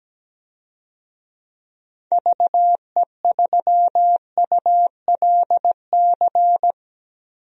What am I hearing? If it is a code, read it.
Morse code, heard as a keyed tone, about 17 wpm, 700 Hz. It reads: VE3ULC
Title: DXCC-100 Call Signs - 6 Characters - Repeated 2x - 17wpm